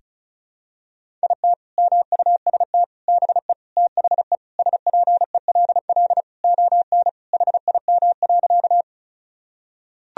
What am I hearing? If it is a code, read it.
Morse code, heard as a keyed tone, about 36 wpm, 700 Hz. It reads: IT MUST BE THE SPELL ON HIM.
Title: Sentences from Top 1000 Words - Review Part 1 - 35wpm